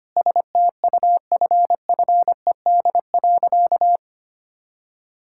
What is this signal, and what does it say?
Morse code, heard as a keyed tone, about 25 wpm, 700 Hz. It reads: STUFFED.
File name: practice / 25wpm / All-Age Appropriate Jokes - 25wpm